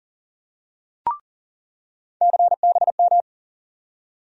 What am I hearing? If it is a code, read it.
Morse code, heard as a keyed tone, about 38 wpm, 700 Hz. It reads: CBM
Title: USA Call Sign Postfixes - 3 Letters 40wpm